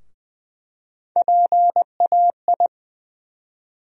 Morse code, heard as a keyed tone, about 20 wpm, 700 Hz. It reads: PAI
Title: USA Call Sign Postfixes - 3 Letters 20wpm